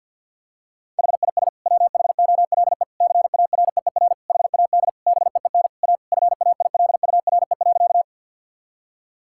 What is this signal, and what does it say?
Morse code, heard as a keyed tone, about 50 wpm, 700 Hz. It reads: HIS WHOLE CAREER HAD BEEN A FAILURE.
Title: Sentences from Top 1000 Words - Review Part 6 - 50wpm